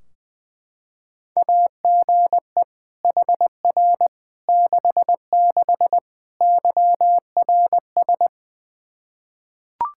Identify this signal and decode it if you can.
Morse code, heard as a keyed tone, about 20 wpm, 700 Hz. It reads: AGE HR 66 YRS
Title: Practice QSO - Exchange 3 - More Info - 20wpm